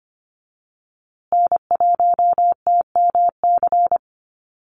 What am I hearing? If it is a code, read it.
Morse code, heard as a keyed tone, about 25 wpm, 700 Hz. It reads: N1TMC
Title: USA Call Signs Repeated 2x 25wpm